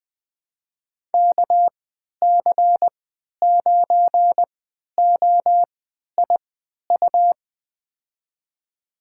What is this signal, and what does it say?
Morse code, heard as a keyed tone, about 20 wpm, spaced as for 11 wpm, 700 Hz. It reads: KC9OIU
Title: USA Call Signs - 6 Characters - 3x Character Spacing - 20wpm